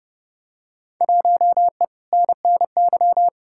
Morse code, heard as a keyed tone, about 30 wpm, 700 Hz. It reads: 1E NNY
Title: Field Day Exchanges - 30wpm